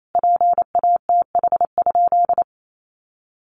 Morse code, heard as a keyed tone, about 28 wpm, 700 Hz. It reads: PATH?
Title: Sentences from Top 1400 Words 28wpm